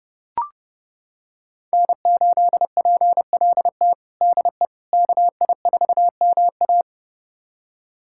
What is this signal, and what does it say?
Morse code, heard as a keyed tone, about 30 wpm, 700 Hz. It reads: N8PLT DE KI4MA
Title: USA Callsign DE Callsign - 5 Characters - 30wpm